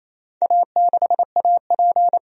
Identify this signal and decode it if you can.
Morse code, heard as a keyed tone, about 27 wpm, 700 Hz. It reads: A6AP
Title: DXCC-100 Call Signs - 4 Characters - 28wpm